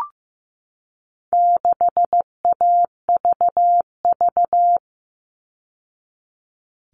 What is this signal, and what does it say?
Morse code, heard as a keyed tone, about 15 wpm, 700 Hz. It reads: 6AVV
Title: All Cut Numbers - 4-Digit 15wpm